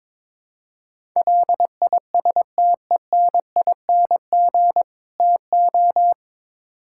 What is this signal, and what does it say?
Morse code, heard as a keyed tone, about 21 wpm, 700 Hz. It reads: LISTENING TO
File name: practice / 22wpm / Build A Phrase - 5-Word Phrases - 22wpm